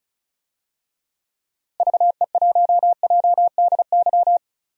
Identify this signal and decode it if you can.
Morse code, heard as a keyed tone, about 34 wpm, 700 Hz. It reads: VE1JDY
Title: DXCC-100 Call Signs - 6 Characters - Repeated 2x - 35wpm